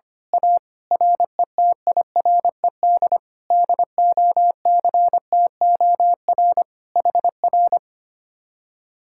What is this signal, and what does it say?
Morse code, heard as a keyed tone, about 24 wpm, 700 Hz. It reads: A RETIRED DOCTOR HR
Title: Speed-Racing Common QSO Phrases - 25wpm